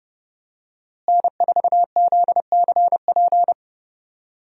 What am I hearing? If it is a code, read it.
Morse code, heard as a keyed tone, about 30 wpm, 700 Hz. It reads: N4ZCP
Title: USA Call Signs - 5 Characters - Repeated 2x - 30wpm